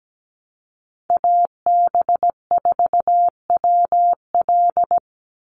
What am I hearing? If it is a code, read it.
Morse code, heard as a keyed tone, about 17 wpm, 700 Hz. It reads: AB4WL
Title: NA SOTA Chaser Callsigns 17wpm